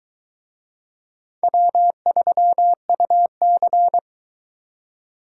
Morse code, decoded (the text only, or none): W3UC